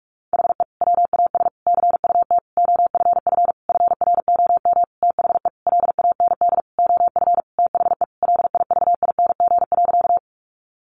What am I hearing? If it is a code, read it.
Morse code, heard as a keyed tone, about 44 wpm, 700 Hz. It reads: HE WAS CUT OFF FROM THE LAND OF THE LIVING.
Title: Sentences from Top 1100 Words 45wpm